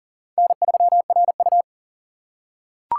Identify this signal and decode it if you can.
Morse code, heard as a keyed tone, about 40 wpm, 700 Hz. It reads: N3RU
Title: USA Call Signs - 4 Characters - 3x Character Spacing - 40wpm